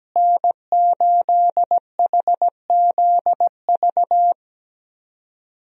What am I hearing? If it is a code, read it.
Morse code, heard as a keyed tone, about 17 wpm, 700 Hz. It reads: N8HZV
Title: USA Call Signs - 5 Characters - Repeated 3x - 17wpm